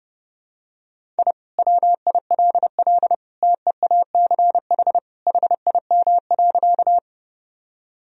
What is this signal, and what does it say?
Morse code, heard as a keyed tone, about 30 wpm, 700 Hz. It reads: I WILL TEACH HIM.
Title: Sentences from Top 1000 Words - Review Part 2 - 30wpm